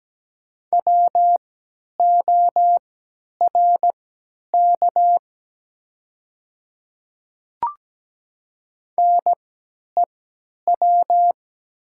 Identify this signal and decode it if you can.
Morse code, heard as a keyed tone, about 17 wpm, spaced as for 10 wpm, 700 Hz. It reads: WORK NEW
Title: 3x Character Spacing - Top 200 Words - 17wpm